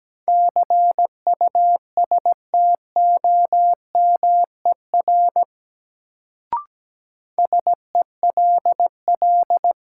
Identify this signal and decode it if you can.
Morse code, heard as a keyed tone, about 17 wpm, 700 Hz. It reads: CUSTOMER SELL
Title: Sentences from Top 1100 Words - New Word Drill Alt - 17wpm